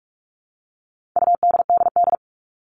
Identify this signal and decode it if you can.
Morse code, heard as a keyed tone, about 44 wpm, 700 Hz. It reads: UDDD
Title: All Cut Numbers - 4-Digit 45wpm